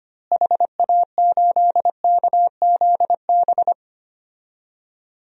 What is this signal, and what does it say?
Morse code, heard as a keyed tone, about 25 wpm, 700 Hz. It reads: HA8KZB
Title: DXCC-100 Call Signs - 6 Characters - Repeated 3x - 25wpm